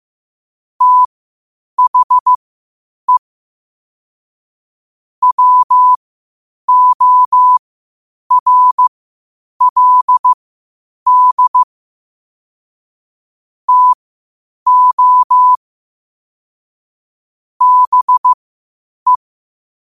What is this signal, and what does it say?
Morse code, heard as a keyed tone, about 15 wpm, spaced as for 8 wpm, 1000 Hz. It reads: THE WORLD TO BE
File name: qotdfiles / 8wpm